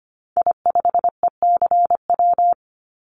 Morse code, heard as a keyed tone, about 25 wpm, 700 Hz. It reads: I5ECW
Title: Call Signs - 25wpm